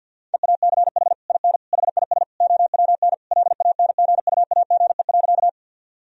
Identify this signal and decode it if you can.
Morse code, heard as a keyed tone, about 50 wpm, 700 Hz. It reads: EACH IN HIS OWN LANGUAGE.
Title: Sentences from Top 1000 Words - Review Part 3 - 50wpm